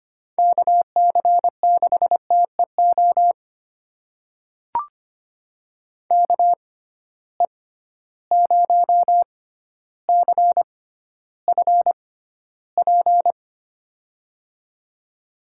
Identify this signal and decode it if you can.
Morse code, heard as a keyed tone, about 25 wpm, spaced as for 13 wpm, 700 Hz. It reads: KC6TEO KE0CFP
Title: USA Call Signs - 6 Characters - 6x Character Spacing - 25wpm